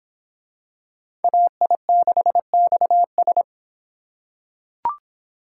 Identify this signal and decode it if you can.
Morse code, heard as a keyed tone, about 25 wpm, 700 Hz. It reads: AI6XS